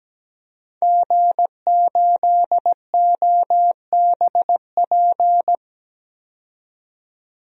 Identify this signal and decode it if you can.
Morse code, heard as a keyed tone, about 17 wpm, 700 Hz. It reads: G8OBP
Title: DXCC-100 Call Signs - 5 Characters - 17wpm